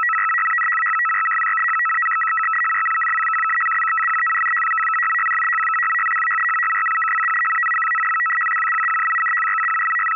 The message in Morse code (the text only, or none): none